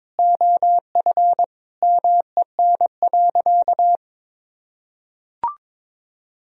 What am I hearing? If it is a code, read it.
Morse code, heard as a keyed tone, about 21 wpm, 700 Hz. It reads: OF MEN.
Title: Sentences from Top 1000 Words - Review Part 1 - 22wpm